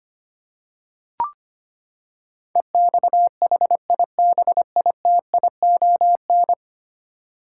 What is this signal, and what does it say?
Morse code, heard as a keyed tone, about 25 wpm, 700 Hz. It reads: EXHIBITION
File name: Ten Letter Words - 25wpm